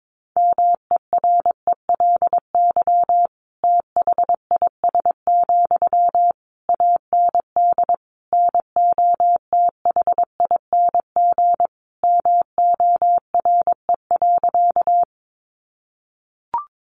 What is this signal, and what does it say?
Morse code, heard as a keyed tone, about 22 wpm, 700 Hz. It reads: MERELY THIS, AND NOTHING MORE.
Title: Sentences from Top 1400 Words 22wpm